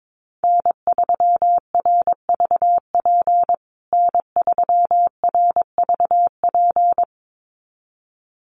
Morse code, heard as a keyed tone, about 22 wpm, 700 Hz. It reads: N3RVP N3RVP
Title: USA Call Signs Repeated 2x 22wpm